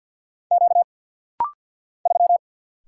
Morse code, heard as a keyed tone, about 50 wpm, 700 Hz. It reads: Q 3
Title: Single Letter-Number - Rapid-Fire 50wpm